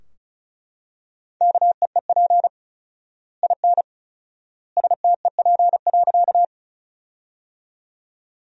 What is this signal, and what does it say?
Morse code, heard as a keyed tone, about 36 wpm, 700 Hz. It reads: KEEP IN STEP.
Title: 4x Word-Spacing - Sentences from Top 400 Words - 35wpm